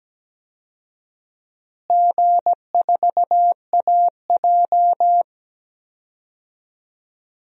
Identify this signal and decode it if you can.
Morse code, heard as a keyed tone, about 17 wpm, 700 Hz. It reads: G4AJ
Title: DXCC-100 Call Signs - 4 Characters - Repeated 2x - 17wpm